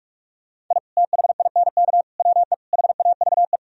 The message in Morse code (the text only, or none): I THINK WE HAVE